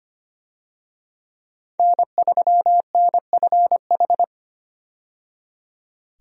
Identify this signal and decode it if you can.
Morse code, heard as a keyed tone, about 25 wpm, 700 Hz. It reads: N3NFH
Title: USA Call Signs - 5 Characters - 25wpm